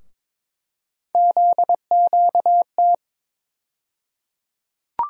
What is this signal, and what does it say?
Morse code, heard as a keyed tone, about 21 wpm, 700 Hz. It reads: ZQT
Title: USA Call Sign Postfixes - 3 Letters 22wpm